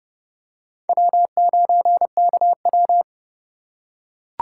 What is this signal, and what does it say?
Morse code, heard as a keyed tone, about 27 wpm, 700 Hz. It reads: W9KW